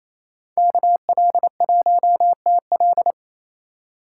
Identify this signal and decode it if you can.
Morse code, heard as a keyed tone, about 36 wpm, 700 Hz. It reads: KL1TL